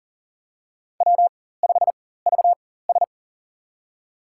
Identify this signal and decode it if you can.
Morse code, heard as a keyed tone, about 40 wpm, spaced as for 19 wpm, 700 Hz. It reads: W5VS
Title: USA Call Signs - 4x Character Spacing - 40wpm